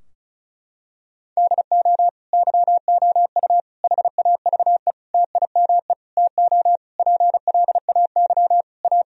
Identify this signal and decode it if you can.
Morse code, heard as a keyed tone, about 36 wpm, 700 Hz. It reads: DO YOU HAVE TIME TO PLAY A
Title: Sentences from Top 1000 Words - Review Part 4 - 35wpm